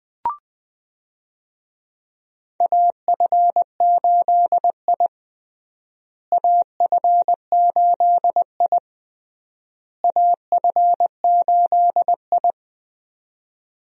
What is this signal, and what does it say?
Morse code, heard as a keyed tone, about 20 wpm, 700 Hz. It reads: AF8I AF8I AF8I
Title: USA Call Signs - 4 Characters - Repeated 3x - 20wpm